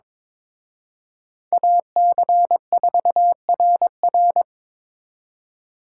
Morse code, heard as a keyed tone, about 26 wpm, 700 Hz. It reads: AC4RR
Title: Speed Racing - USA Call Signs 22wpm